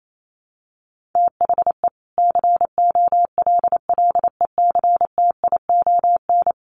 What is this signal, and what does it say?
Morse code, heard as a keyed tone, about 28 wpm, 700 Hz. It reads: THE COLLECTION
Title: Sentences from Top 1200 Words 28wpm